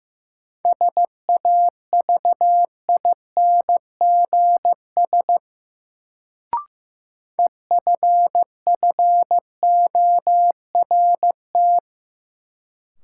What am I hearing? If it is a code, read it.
Morse code, heard as a keyed tone, about 15 wpm, 700 Hz. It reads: SAVINGS EFFORT
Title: Sentences from Top 1200 Words - New Word Drill - 15wpm